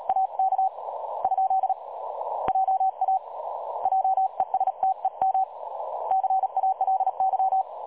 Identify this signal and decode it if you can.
Morse code, heard as a keyed tone, about 36 wpm, 760 Hz. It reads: AK ? JA JESTEM CALY
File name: dlazaawansowanych / nagraniazpasma / websdr recording 3565.0kHz